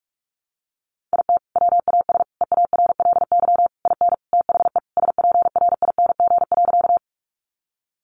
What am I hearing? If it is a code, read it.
Morse code, heard as a keyed tone, about 46 wpm, 700 Hz. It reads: IT WAS EARLY IN THE SPRING.
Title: Sentences from Top 1200 Words 45wpm